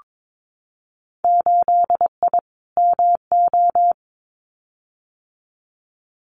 Morse code, heard as a keyed tone, about 22 wpm, 700 Hz. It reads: 8I MO
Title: Winter Field Day Exchanges Encore 22wpm